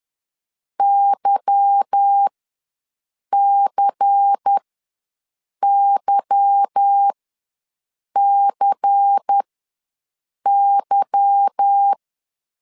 Morse code, heard as a keyed tone, about 10 wpm, spaced as for 6 wpm, 800 Hz. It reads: YCYCY
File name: LEZIONI / 8 YJ45 / 8 Y ws 10wpm 3952